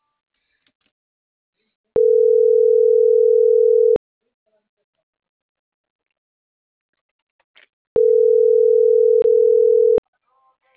Ringback tone (North America), 440 + 480 Hz.